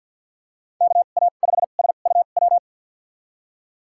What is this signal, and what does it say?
Morse code, heard as a keyed tone, about 52 wpm, spaced as for 36 wpm, 700 Hz. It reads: KA5SUW
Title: USA Call Signs - 2x Character Spacing - 50wpm